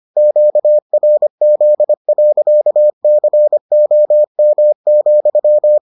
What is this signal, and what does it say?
Morse code, heard as a keyed tone, about 25 wpm, 600 Hz. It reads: QRZ.COM,